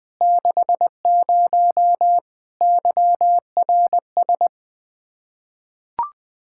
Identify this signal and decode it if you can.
Morse code, heard as a keyed tone, about 20 wpm, 700 Hz. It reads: 60 YRS
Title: Practice QSO - Exchange 3 - More Info - 20wpm